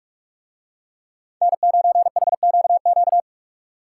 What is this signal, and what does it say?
Morse code, heard as a keyed tone, about 46 wpm, 700 Hz. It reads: N9HQQ